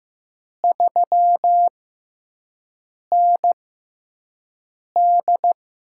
Morse code, heard as a keyed tone, about 15 wpm, spaced as for 5 wpm, 700 Hz. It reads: 3ND
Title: USA Call Signs - 6 Characters - 6x Character Spacing - 15wpm